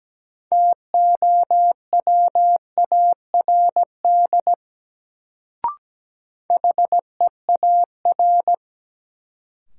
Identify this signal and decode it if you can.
Morse code, heard as a keyed tone, about 17 wpm, 700 Hz. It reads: TOWARD HEAR
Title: Sets of 5 Words - New Word Drill - 17wpm